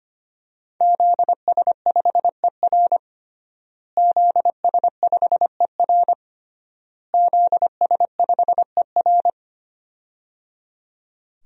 Morse code, heard as a keyed tone, about 25 wpm, 700 Hz. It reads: ZS5ER ZS5ER ZS5ER
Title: DXCC-100 Call Signs - 5 Characters - Repeated 3x - 25wpm